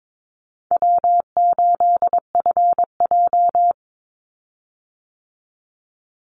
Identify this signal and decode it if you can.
Morse code, heard as a keyed tone, about 22 wpm, 700 Hz. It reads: W8FJ
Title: Easy Call Signs Repeated 2x - 22wpm